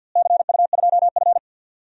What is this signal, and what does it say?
Morse code, heard as a keyed tone, about 50 wpm, 700 Hz. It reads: CU2F
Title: DXCC-100 Call Signs - 4 Characters - Repeated 3x - 50wpm